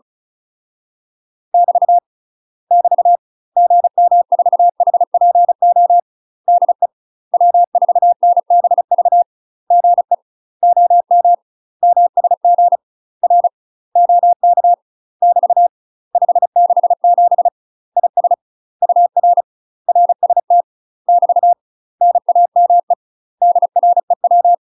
Morse code, heard as a keyed tone, about 35 wpm, 700 Hz. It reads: <BT> <BT> GM4HPO DE W4NBV GE OM MSG R OK <BT> 567 IS UR RST <BT> NAME DREW
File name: UK MASTER QSOs TRAINING / 35WPM / UK MASTER QSOs 0030